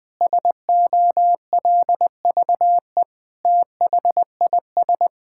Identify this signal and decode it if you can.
Morse code, heard as a keyed tone, about 21 wpm, 700 Hz. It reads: SOLVE THIS
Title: Sentences with 7-Word Phrases - 20wpm